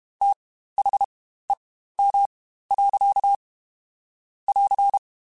Morse code, heard as a keyed tone, about 32 wpm, spaced as for 15 wpm, 800 Hz. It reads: THEM. <AR>